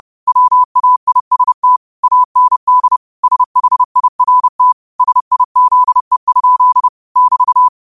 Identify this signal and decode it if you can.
Morse code, heard as a keyed tone, about 30 wpm, 1000 Hz. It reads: WAIST AND SHIRT SIZE? <BT>